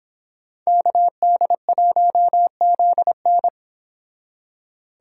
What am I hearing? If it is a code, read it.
Morse code, heard as a keyed tone, about 25 wpm, 700 Hz. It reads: KD1ZN